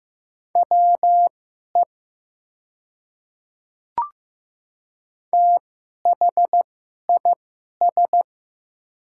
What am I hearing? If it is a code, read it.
Morse code, heard as a keyed tone, about 15 wpm, spaced as for 11 wpm, 700 Hz. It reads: WE THIS
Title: 2x Character Spacing - Top 100 Words - 15wpm